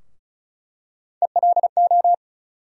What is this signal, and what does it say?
Morse code, heard as a keyed tone, about 34 wpm, 700 Hz. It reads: ELO
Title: USA Call Sign Postfixes - 3 Letters 35wpm